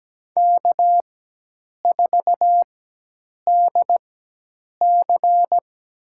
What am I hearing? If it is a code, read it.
Morse code, heard as a keyed tone, about 17 wpm, spaced as for 8 wpm, 700 Hz. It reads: K4DC
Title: USA Call Signs - 4 Characters - 4x Character Spacing - 17wpm